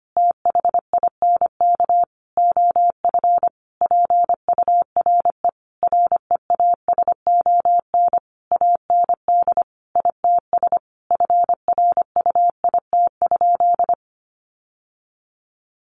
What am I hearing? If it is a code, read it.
Morse code, heard as a keyed tone, about 25 wpm, 700 Hz. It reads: THINK OF PURE REASON AND ITS FRUIT?
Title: Sentences from Top 1500 Words 25wpm